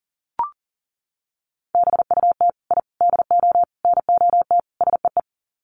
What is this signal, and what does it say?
Morse code, heard as a keyed tone, about 40 wpm, 700 Hz. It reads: BUT I DO NOT SEE
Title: Sets of 5 Words Encore 40wpm